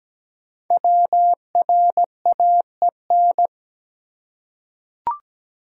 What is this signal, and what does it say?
Morse code, heard as a keyed tone, about 17 wpm, 700 Hz. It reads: WRAEN
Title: Code Groups - 5 Letters - 17wpm